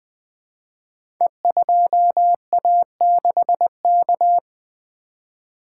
Morse code, heard as a keyed tone, about 20 wpm, 700 Hz. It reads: E2A6K
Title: Code Groups - 5 Letter:Numbers - 20wpm